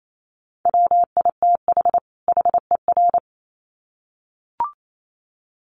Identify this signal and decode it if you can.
Morse code, heard as a keyed tone, about 28 wpm, 700 Hz. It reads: WITH HER
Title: Sets of 2 Words 28wpm